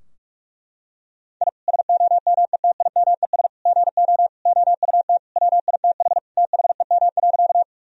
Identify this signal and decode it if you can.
Morse code, heard as a keyed tone, about 41 wpm, 700 Hz. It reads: I SOMETIMES GO OUT WITH THEM.
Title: Sentences from Top 1000 Words - Review Part 3 - 45wpm